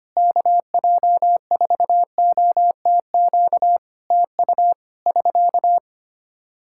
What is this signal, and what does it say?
Morse code, heard as a keyed tone, about 25 wpm, 700 Hz. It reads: KJ4OTQ TU <SK>
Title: Practice QSO - Exchange 4 - Ending - 25wpm